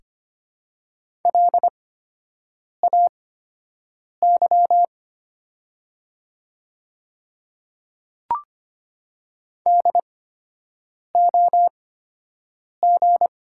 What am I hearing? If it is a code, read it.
Morse code, heard as a keyed tone, about 25 wpm, spaced as for 7 wpm, 700 Hz. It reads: LAY DOG
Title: 8x Character Spacing - Top 500 Words - 25wpm